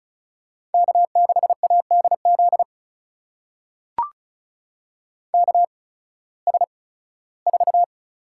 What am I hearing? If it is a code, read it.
Morse code, heard as a keyed tone, about 34 wpm, spaced as for 18 wpm, 700 Hz. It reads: K6ADZ KS4
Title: USA Call Signs - 8x Character Spacing - 35wpm